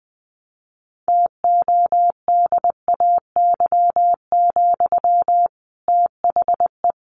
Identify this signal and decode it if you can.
Morse code, heard as a keyed tone, about 19 wpm, 700 Hz. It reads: TODAY, THE
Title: Sentences from Top 1400 Words 20wpm